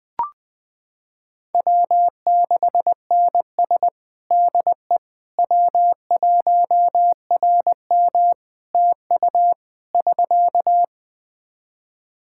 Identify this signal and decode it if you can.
Morse code, heard as a keyed tone, about 21 wpm, 700 Hz. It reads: W6NS DE W1RM TU <SK>
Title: Practice QSO - Exchange 4 - Ending - 20wpm